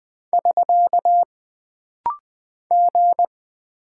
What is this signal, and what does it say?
Morse code, heard as a keyed tone, about 20 wpm, 700 Hz. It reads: <SK> G
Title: Single Character (Letter:Number:Punctuation:Prosign) - Rapid-Fire - 20wpm